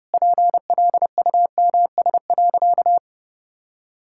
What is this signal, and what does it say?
Morse code, heard as a keyed tone, about 32 wpm, 700 Hz. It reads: PLUMS.
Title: Pangram Challenge v2 - 30wpm